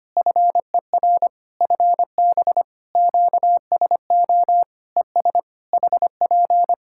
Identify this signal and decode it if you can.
Morse code, heard as a keyed tone, about 25 wpm, 700 Hz. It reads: FER FB QSO ES HP
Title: K1USN SST QSO Practice 25wpm